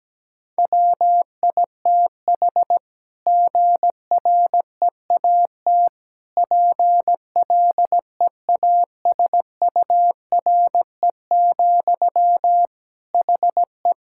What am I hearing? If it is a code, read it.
Morse code, heard as a keyed tone, about 17 wpm, 700 Hz. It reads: WITH GREAT PLEASURE, HE
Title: Sentences from Top 1600 Words 17wpm